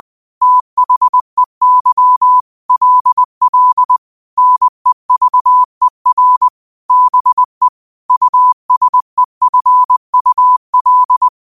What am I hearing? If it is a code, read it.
Morse code, heard as a keyed tone, about 20 wpm, 1000 Hz. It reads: THEY LL NEVER BE USEFUL
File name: qotdfiles / 20wpm